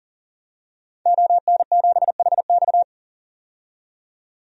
Morse code, heard as a keyed tone, about 40 wpm, 700 Hz. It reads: ON7HX
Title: DXCC-100 Call Signs - 5 Characters - 40wpm